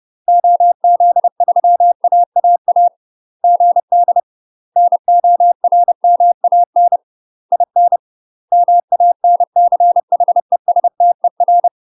Morse code, heard as a keyed tone, about 30 wpm, 700 Hz. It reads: OZ3AAA GD NORMAN IN MANCHESTER
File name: UK MASTER QSOs TRAINING / 30WPM / UK MASTER QSOs 0046